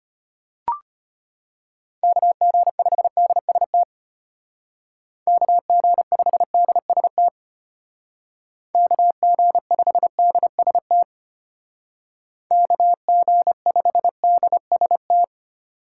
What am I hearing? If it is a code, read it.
Morse code, heard as a keyed tone, about 33 wpm, 700 Hz. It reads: KG5DST KG5DST KG5DST KG5DST